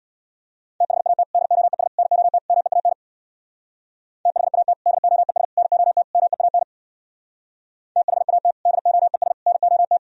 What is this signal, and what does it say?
Morse code, heard as a keyed tone, about 64 wpm, 700 Hz. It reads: THAT DOES NOT MEAN THAT DOES NOT MEAN THAT DOES NOT